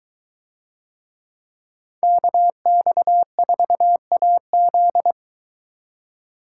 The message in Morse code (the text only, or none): KX4AZ